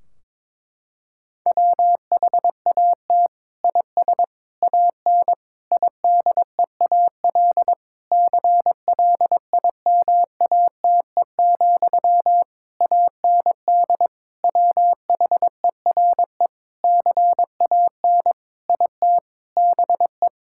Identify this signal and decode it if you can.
Morse code, heard as a keyed tone, about 22 wpm, 700 Hz. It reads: WHAT IS AN IDEAL CLIMATE, AND WHERE CAN IT BE
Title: Sentences from Top 1600 Words 22wpm